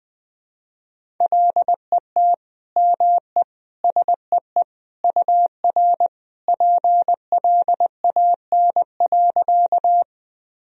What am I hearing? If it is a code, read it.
Morse code, heard as a keyed tone, about 21 wpm, 700 Hz. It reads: LET ME SEE UR PLAN.
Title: Sentences from Top 500 Words and Abbreviations 20wpm